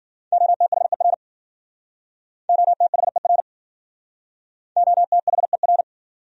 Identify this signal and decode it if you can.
Morse code, heard as a keyed tone, about 53 wpm, 700 Hz. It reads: OTHER OTHER OTHER